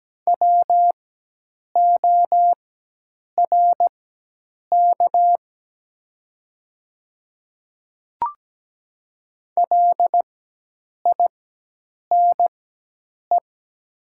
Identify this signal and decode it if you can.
Morse code, heard as a keyed tone, about 17 wpm, spaced as for 8 wpm, 700 Hz. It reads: WORK LINE
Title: 4x Character Spacing - Top 200 Words - 17wpm